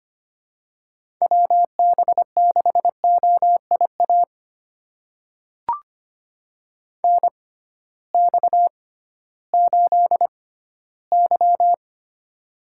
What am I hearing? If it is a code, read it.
Morse code, heard as a keyed tone, about 25 wpm, spaced as for 15 wpm, 700 Hz. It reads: WB6OIA NX8Y
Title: USA Call Signs - 6x Character Spacing - 25wpm